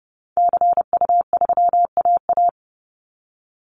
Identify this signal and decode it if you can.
Morse code, heard as a keyed tone, about 30 wpm, 700 Hz. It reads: CU3AA
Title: Call Signs Repeated 3x - 30wpm